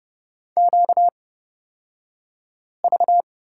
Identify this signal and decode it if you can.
Morse code, heard as a keyed tone, about 30 wpm, spaced as for 5 wpm, 700 Hz. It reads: QV
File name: Lesson 38d - Letter q - Two Characters 30wpm